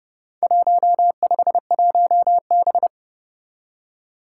Morse code, heard as a keyed tone, about 30 wpm, 700 Hz. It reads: 151B
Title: Four Challenging Letter-Number Combinations 30wpm